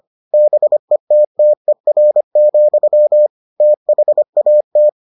Morse code, heard as a keyed tone, about 25 wpm, 600 Hz. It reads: BETTER, THAT